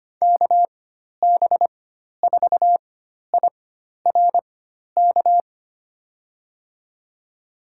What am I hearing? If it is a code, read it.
Morse code, heard as a keyed tone, about 25 wpm, spaced as for 12 wpm, 700 Hz. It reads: KB4IRK
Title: USA Call Signs - 6 Characters - 4x Character Spacing - 25wpm